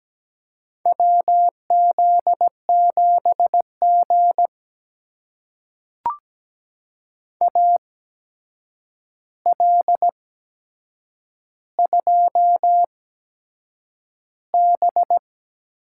Morse code, heard as a keyed tone, about 17 wpm, spaced as for 7 wpm, 700 Hz. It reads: WZ7G AL2B
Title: USA Call Signs - 4 Characters - 8x Character Spacing - 17wpm